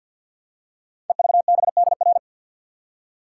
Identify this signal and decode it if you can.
Morse code, heard as a keyed tone, about 50 wpm, 700 Hz. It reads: EVBDR
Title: Code Groups - 5 Letter:Numbers - 50wpm